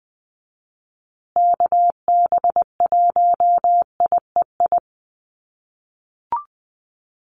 Morse code, heard as a keyed tone, about 20 wpm, 700 Hz. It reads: KB1IEI
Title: USA Call Signs Repeated 2x 20wpm